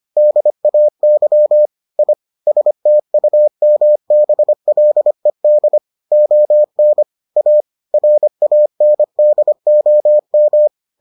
Morse code, heard as a keyed tone, about 25 wpm, 600 Hz. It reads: DAY I STUMBLED ON A RANDOM